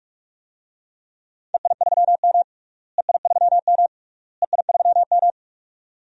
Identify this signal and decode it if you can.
Morse code, heard as a keyed tone, about 42 wpm, 700 Hz. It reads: EI3M EI3M EI3M
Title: DXCC-100 Call Signs - 4 Characters - Repeated 3x - 45wpm